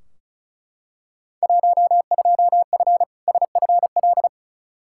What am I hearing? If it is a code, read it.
Morse code, heard as a keyed tone, about 34 wpm, 700 Hz. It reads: 12F SFL
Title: Field Day Exchanges Encore - 35wpm